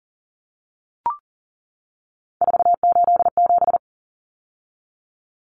Morse code, heard as a keyed tone, about 40 wpm, 700 Hz. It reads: 487
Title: Common Cut Numbers - 3-Digit 40wpm